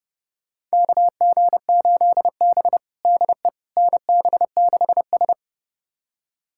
Morse code, heard as a keyed tone, about 30 wpm, 700 Hz. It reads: KG8B DE NB6S
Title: USA Callsign DE Callsign - 4 Characters - 30wpm